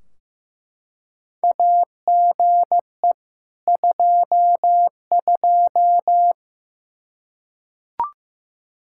Morse code, heard as a keyed tone, about 15 wpm, 700 Hz. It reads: AGE 22
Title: Practice QSO - Exchange 3 - More Info - 15wpm